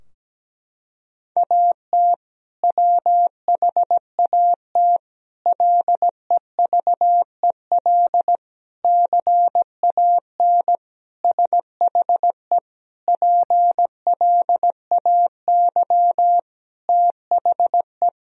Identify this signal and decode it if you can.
Morse code, heard as a keyed tone, about 17 wpm, 700 Hz. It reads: AT WHAT LEVEL CAN SHE PLAY THE
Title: Sentences from Top 1000 Words - Review Part 4 - 17wpm